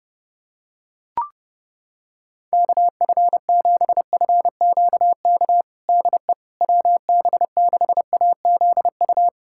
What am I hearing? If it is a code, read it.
Morse code, heard as a keyed tone, about 30 wpm, 700 Hz. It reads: KF7FQK DE WB6AZU
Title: USA Callsign DE Callsign - 6 Characters - 30wpm